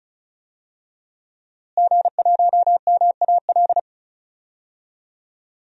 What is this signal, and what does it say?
Morse code, heard as a keyed tone, about 36 wpm, 700 Hz. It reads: G1MAL
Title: DXCC-100 Call Signs - 5 Characters - Repeated 3x - 35wpm